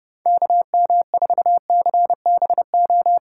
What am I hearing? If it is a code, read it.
Morse code, heard as a keyed tone, about 31 wpm, 700 Hz. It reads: KM4CBO